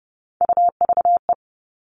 Morse code, heard as a keyed tone, about 30 wpm, 700 Hz. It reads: UVE
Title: All Cut Numbers - 3-Digit 30wpm